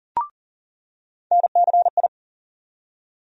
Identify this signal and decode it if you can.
Morse code, heard as a keyed tone, about 40 wpm, 700 Hz. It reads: NCI
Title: USA Call Sign Postfixes - 3 Letters 40wpm